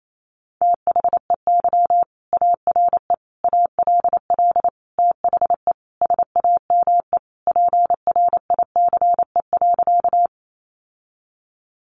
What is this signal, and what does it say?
Morse code, heard as a keyed tone, about 28 wpm, 700 Hz. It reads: THEY ARE ALL THE SAME PRICE.
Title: Sentences from Top 600 Words 28wpm